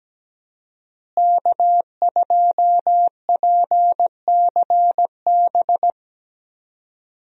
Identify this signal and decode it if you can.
Morse code, heard as a keyed tone, about 17 wpm, 700 Hz. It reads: K2PCB
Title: USA Call Signs - 5 Characters - Repeated 2x - 17wpm